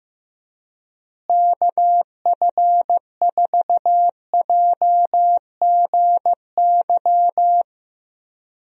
Morse code, heard as a keyed tone, about 15 wpm, 700 Hz. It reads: KF4JGY
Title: USA Call Signs - 6 Characters - Repeated 2x - 15wpm